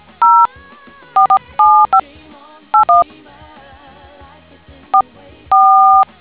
DTMF, heard as telephone touch tones, tones of about 200 ms. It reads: 047581 44